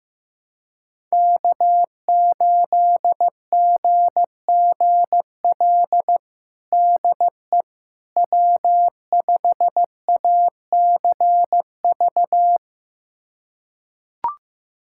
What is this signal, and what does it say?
Morse code, heard as a keyed tone, about 15 wpm, 700 Hz. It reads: K8GGL DE W5ACV
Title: USA Callsign DE Callsign - 5 Characters - 15wpm